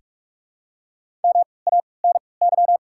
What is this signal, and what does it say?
Morse code, heard as a keyed tone, about 43 wpm, spaced as for 25 wpm, 700 Hz. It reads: MANY
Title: 3x Character Spacing - Top 100 Words - 45wpm